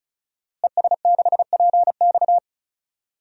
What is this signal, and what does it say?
Morse code, heard as a keyed tone, about 36 wpm, 700 Hz. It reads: ES6PX
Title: DXCC-100 Call Signs - 5 Characters - Repeated 3x - 35wpm